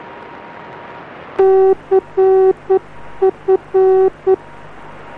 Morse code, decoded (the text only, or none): CF